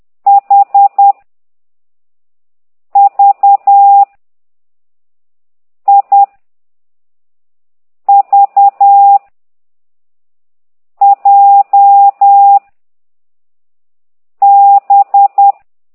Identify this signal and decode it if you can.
Morse code, heard as a keyed tone, about 10 wpm, 800 Hz. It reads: H V I V J B